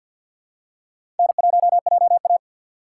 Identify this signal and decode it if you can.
Morse code, heard as a keyed tone, about 47 wpm, 700 Hz. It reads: N1JA
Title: USA Call Signs - 4 Characters - 2x Character Spacing - 50wpm